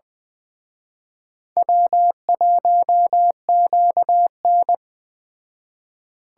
Morse code, heard as a keyed tone, about 20 wpm, 700 Hz. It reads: W1QN
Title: Speed Racing - USA 1x2 Call Signs 17wpm